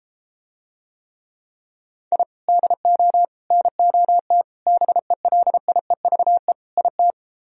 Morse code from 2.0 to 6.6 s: I DO NOT BELIEVE

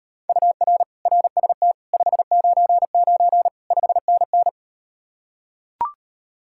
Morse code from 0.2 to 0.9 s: UR

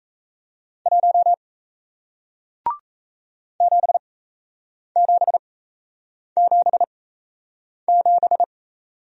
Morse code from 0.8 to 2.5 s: 1

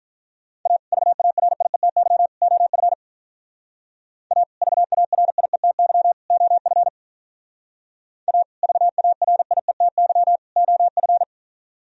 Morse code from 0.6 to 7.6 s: A VARIETY OF A VARIETY OF

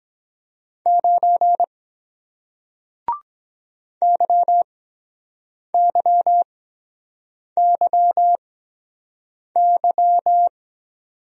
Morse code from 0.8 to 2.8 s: 9

